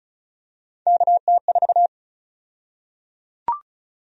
Morse code, heard as a keyed tone, about 34 wpm, 700 Hz. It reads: KT4
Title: USA Call Sign Prefixes - Repeated 2x - 35wpm